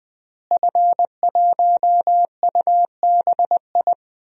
Morse code, heard as a keyed tone, about 20 wpm, 700 Hz. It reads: F1UBI
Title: DXCC-100 Call Signs - 5 Characters - Repeated 3x - 20wpm